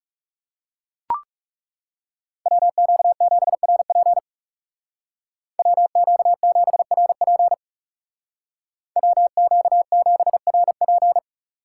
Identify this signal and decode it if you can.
Morse code, heard as a keyed tone, about 36 wpm, 700 Hz. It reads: WQ7RP WQ7RP WQ7RP
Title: Speed Racing - USA 2x2 Call Signs 30wpm